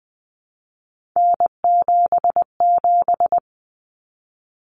Morse code, heard as a keyed tone, about 20 wpm, 700 Hz. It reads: N77
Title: Common Cut Numbers - 3-Digit 20wpm